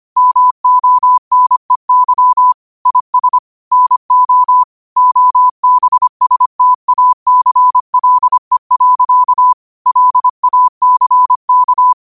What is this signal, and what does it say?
Morse code, heard as a keyed tone, about 25 wpm, 1000 Hz. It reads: MONEY IS NO OBSTACLE. LACK
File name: qotdfiles / 25wpm